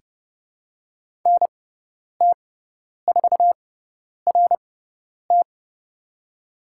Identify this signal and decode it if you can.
Morse code, heard as a keyed tone, about 30 wpm, spaced as for 10 wpm, 700 Hz. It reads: NT4RT
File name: Lesson 16g - Letter c - Call Signs 30wpm